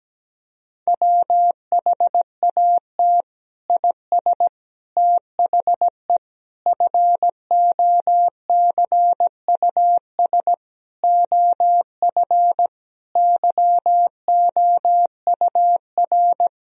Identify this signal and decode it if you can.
Morse code, heard as a keyed tone, about 17 wpm, 700 Hz. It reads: WHAT IS THE FOCUS OF YOUR
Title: Sentences from Top 1000 Words - Review Part 4 - 17wpm